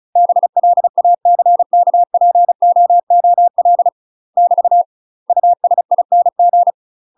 Morse code, heard as a keyed tone, about 36 wpm, 700 Hz. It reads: BLACKPOOL <BT> USING
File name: UK MASTER QSOs TRAINING / 35WPM / UK MASTER QSOs 0031